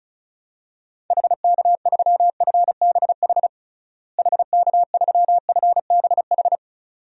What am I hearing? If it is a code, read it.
Morse code, heard as a keyed tone, about 36 wpm, 700 Hz. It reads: HK3FBH HK3FBH
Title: DXCC-100 Call Signs - 6 Characters - Repeated 2x - 35wpm